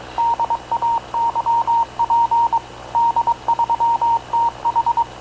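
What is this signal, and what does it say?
Morse code, heard as a keyed tone, about 22 wpm, 930 Hz. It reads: DAYP D3TH
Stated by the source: Havana Radio / CLA sending Spanish press (PX) in Morse, recorded by Radio Officer Paul Dumesnil onboard CNAV QUEST/CZDO.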